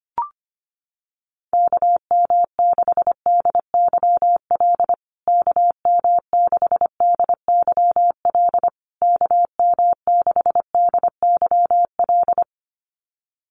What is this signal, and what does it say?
Morse code, heard as a keyed tone, about 25 wpm, 700 Hz. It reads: KM6DYL KM6DYL KM6DYL
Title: USA Call Signs Repeated 3x 25wpm